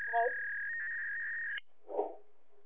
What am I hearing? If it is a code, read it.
Fax machine handshake.